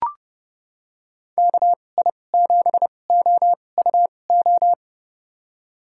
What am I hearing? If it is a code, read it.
Morse code, heard as a keyed tone, about 30 wpm, spaced as for 22 wpm, 700 Hz. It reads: KI7OUO
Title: USA Call Signs - 6 Characters - 2x Character Spacing - 30wpm